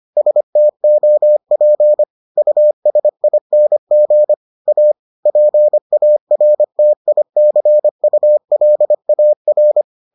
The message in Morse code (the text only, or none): STOP USING A PARTICULAR